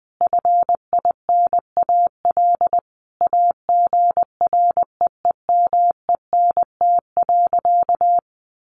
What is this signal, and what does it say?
Morse code, heard as a keyed tone, about 20 wpm, 700 Hz. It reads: FINAL AGREEMENT.
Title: Sentences from Top 1500 Words 20wpm